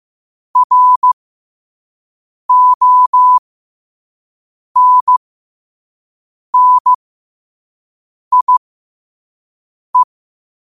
Morse code, heard as a keyed tone, about 15 wpm, spaced as for 5 wpm, 1000 Hz. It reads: RONNIE